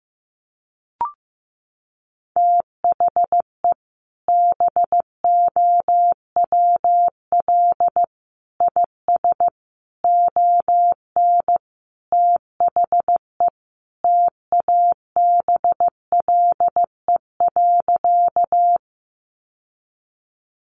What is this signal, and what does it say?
Morse code, heard as a keyed tone, about 15 wpm, 700 Hz. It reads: THE BOWL IS ON THE TABLE.
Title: Sentences from Top 1200 Words 15wpm